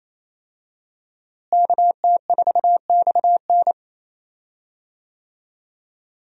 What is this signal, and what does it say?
Morse code, heard as a keyed tone, about 28 wpm, 700 Hz. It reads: KT4XN
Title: USA Call Signs - 5 Characters - Repeated 3x - 28wpm